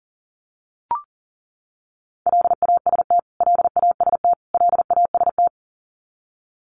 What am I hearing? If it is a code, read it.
Morse code, heard as a keyed tone, about 37 wpm, 700 Hz. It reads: LAST LAST LAST
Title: Top 500 Words Repeated 40wpm